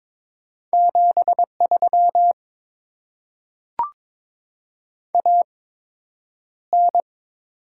Morse code, heard as a keyed tone, about 22 wpm, spaced as for 9 wpm, 700 Hz. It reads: 73 AN
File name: Top 20 QSO Elements - 8x Character Spacing - 22wpm